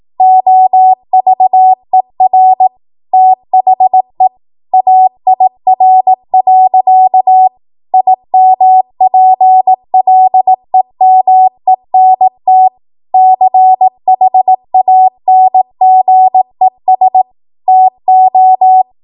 Morse code, heard as a keyed tone, about 18 wpm, 750 Hz. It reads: OVER THE AIR. IMPLEMENT CHANGES TO
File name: Morse / 18WPM